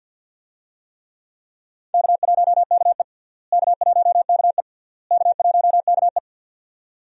Morse code, heard as a keyed tone, about 48 wpm, 700 Hz. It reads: K1KE K1KE K1KE
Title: USA Call Signs - 4 Characters - Repeated 3x - 50wpm